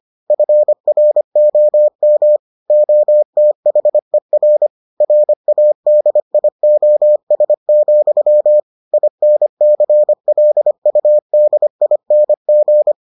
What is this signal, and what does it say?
Morse code, heard as a keyed tone, about 25 wpm, 600 Hz. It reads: FROM OTHER RADIOS, INCLUDING